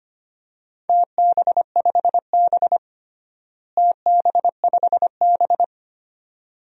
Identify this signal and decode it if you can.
Morse code, heard as a keyed tone, about 25 wpm, 700 Hz. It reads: TB5B TB5B
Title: DXCC-100 Call Signs - 4 Characters - Repeated 2x - 25wpm